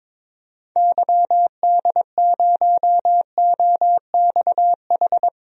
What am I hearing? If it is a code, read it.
Morse code, heard as a keyed tone, about 22 wpm, 700 Hz. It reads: YD0OXH
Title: DXCC-100 Call Signs - 6 Characters - Repeated 2x - 22wpm